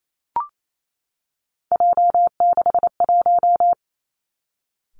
Morse code, heard as a keyed tone, about 28 wpm, 700 Hz. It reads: J61
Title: Three Challenging Letter-Number Combinations 28wpm